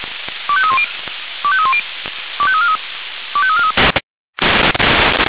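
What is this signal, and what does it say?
Five-tone selective call (ZVEI), heard as quick runs of five tones, tones of about 80 ms. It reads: 25310 25310 25343 25343
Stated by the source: ZVEI five-tone selective call received with an SDR.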